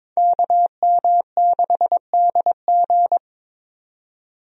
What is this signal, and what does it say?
Morse code, heard as a keyed tone, about 22 wpm, 700 Hz. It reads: KM6DG
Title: USA Call Signs - 5 Characters - 22wpm